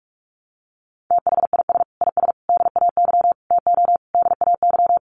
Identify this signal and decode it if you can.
Morse code, heard as a keyed tone, about 44 wpm, 700 Hz. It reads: THIS IS DAY TO DAY
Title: Sets of 5 Words Encore 45wpm